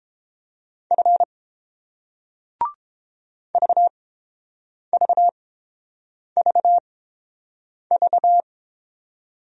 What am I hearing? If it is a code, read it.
Morse code, heard as a keyed tone, about 30 wpm, spaced as for 7 wpm, 700 Hz. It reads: F VVVV